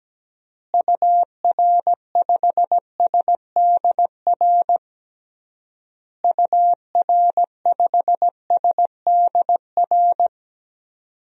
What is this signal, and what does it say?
Morse code, heard as a keyed tone, about 17 wpm, 700 Hz. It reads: UR5SDR UR5SDR
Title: DXCC-100 Call Signs - 6 Characters - Repeated 3x - 17wpm